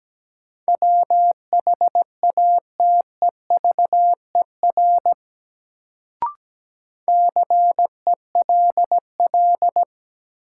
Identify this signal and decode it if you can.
Morse code, heard as a keyed tone, about 17 wpm, 700 Hz. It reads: WHATEVER CELL
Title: Sentences from Top 1100 Words - New Word Drill Alt - 17wpm